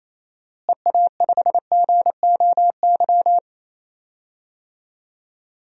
Morse code, heard as a keyed tone, about 28 wpm, 700 Hz. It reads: EA5GOY
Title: DXCC-100 Call Signs - 6 Characters - Repeated 3x - 28wpm